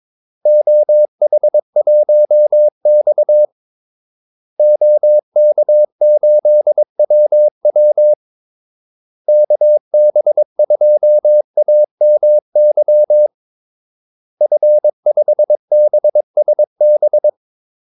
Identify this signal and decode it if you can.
Morse code, heard as a keyed tone, about 22 wpm, 600 Hz. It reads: OH1X OK8WW KB2AMY F5BSB
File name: Callsigns-22wpm